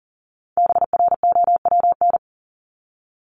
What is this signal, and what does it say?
Morse code, heard as a keyed tone, about 40 wpm, 700 Hz. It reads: BROWN
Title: Top 1001-1250 Words 40wpm